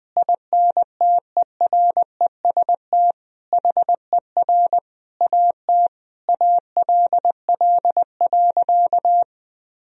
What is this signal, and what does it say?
Morse code, heard as a keyed tone, about 20 wpm, 700 Hz. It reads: INTEREST HER AT ALL.
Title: Sentences from Top 1000 Words - Review Part 2 - 20wpm